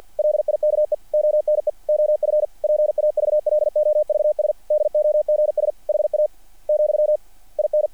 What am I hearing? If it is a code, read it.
Morse code, heard as a keyed tone, about 48 wpm, 660 Hz. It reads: LIKE ONE OF PAVLOVS DOGS HA, IN
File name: archived / CRSnet